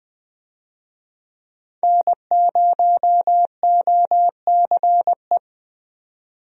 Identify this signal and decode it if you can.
Morse code, heard as a keyed tone, about 20 wpm, 700 Hz. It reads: N0OCE
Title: USA Call Signs - 5 Characters - 20wpm